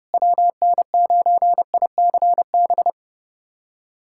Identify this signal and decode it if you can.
Morse code, heard as a keyed tone, about 32 wpm, 700 Hz. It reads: WN9ICB
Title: USA Callsign DE Callsign 30wpm